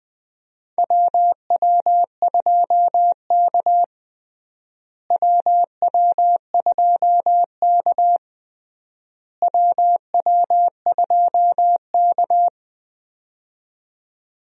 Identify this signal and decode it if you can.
Morse code, heard as a keyed tone, about 20 wpm, 700 Hz. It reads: WW2K WW2K WW2K
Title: USA Call Signs - 4 Characters - Repeated 3x - 20wpm